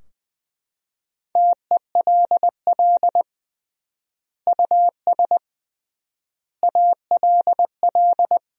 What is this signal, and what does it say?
Morse code, heard as a keyed tone, about 20 wpm, 700 Hz. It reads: TELL US ALL
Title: 3x Word-Spacing - Sentences from Top 200 Words - 20wpm